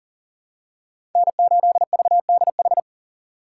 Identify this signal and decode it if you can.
Morse code, heard as a keyed tone, about 40 wpm, 700 Hz. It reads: N8VDH
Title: USA Call Signs - 5 Characters - 4x Character Spacing - 40wpm